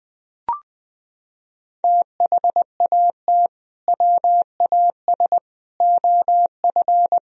Morse code, heard as a keyed tone, about 20 wpm, 700 Hz. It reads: THAT WAS OF
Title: Sentences from Top 1000 Words - Review Part 2 - 20wpm